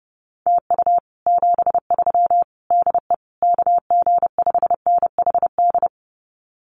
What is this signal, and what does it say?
Morse code, heard as a keyed tone, about 30 wpm, 700 Hz. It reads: TU 73 DE KG5NHD
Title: Common QSO Phrases Encore 30wpm